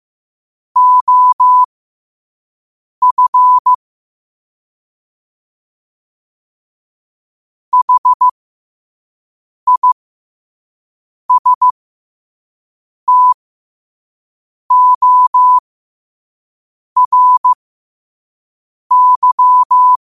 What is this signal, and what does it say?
Morse code, heard as a keyed tone, about 15 wpm, spaced as for 5 wpm, 1000 Hz. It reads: OF HISTORY
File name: qotdfiles / 5wpm